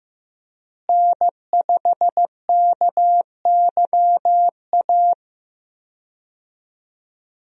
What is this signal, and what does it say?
Morse code, heard as a keyed tone, about 15 wpm, 700 Hz. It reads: N5KYA
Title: USA Call Signs - 5 Characters - Repeated 2x - 15wpm